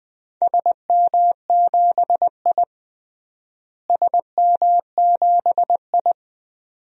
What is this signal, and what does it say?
Morse code, heard as a keyed tone, about 20 wpm, 700 Hz. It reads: SM7I SM7I
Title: DXCC-100 Call Signs - 4 Characters - Repeated 2x - 20wpm